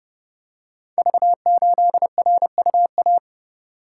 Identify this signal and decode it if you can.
Morse code, heard as a keyed tone, about 28 wpm, 700 Hz. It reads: V8RUA
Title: Code Groups - 5 Letter:Numbers - 30wpm